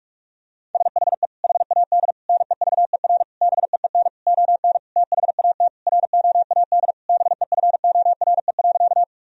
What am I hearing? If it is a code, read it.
Morse code, heard as a keyed tone, about 42 wpm, 700 Hz. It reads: SHE HAD NEVER BEEN ON THAT ROAD BEFORE.
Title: Sentences from Top 1000 Words - Review Part 2 - 45wpm